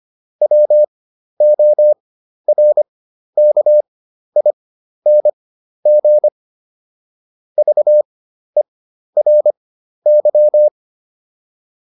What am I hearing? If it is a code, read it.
Morse code, heard as a keyed tone, about 25 wpm, spaced as for 12 wpm, 600 Hz. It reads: WORKING VERY